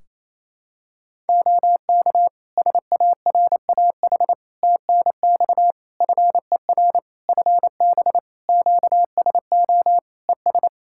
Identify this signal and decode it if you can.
Morse code, heard as a keyed tone, about 28 wpm, 700 Hz. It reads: OK SARAH TNX FER FB QSO ES
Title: Practice QSO - Exchange 4 - Ending - 28wpm